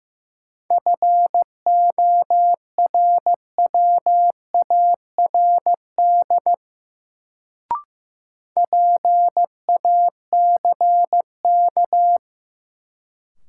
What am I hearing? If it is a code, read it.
Morse code, heard as a keyed tone, about 15 wpm, 700 Hz. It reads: FORWARD PACK
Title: Sentences from Top 1300 Words - New Word Drill Alt - 15wpm